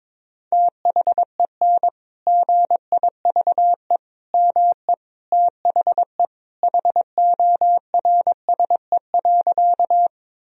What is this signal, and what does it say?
Morse code, heard as a keyed tone, about 22 wpm, 700 Hz. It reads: THEN GIVE ME THE HORSE.
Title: Sentences from Top 1000 Words - Review Part 1 - 22wpm